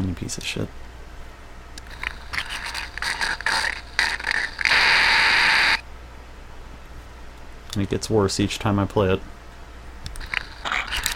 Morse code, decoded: none